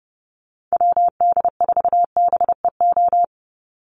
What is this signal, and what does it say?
Morse code, heard as a keyed tone, about 30 wpm, 700 Hz. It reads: WD4BEO
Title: USA Call Signs 30wpm